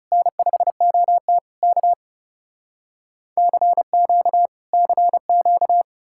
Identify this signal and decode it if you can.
Morse code, heard as a keyed tone, about 40 wpm, 700 Hz. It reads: N5OT K CQ CQ